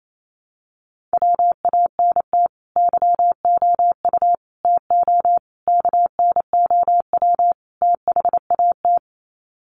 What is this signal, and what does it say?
Morse code, heard as a keyed tone, about 28 wpm, 700 Hz. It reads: WANT YOU TO KNOW THAT
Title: Sets of 5 Words Encore 28wpm